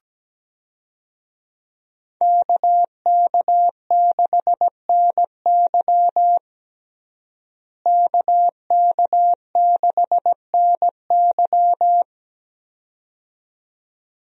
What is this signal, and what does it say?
Morse code, heard as a keyed tone, about 17 wpm, 700 Hz. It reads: KK6NY KK6NY
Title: USA Call Signs - 5 Characters - Repeated 2x - 17wpm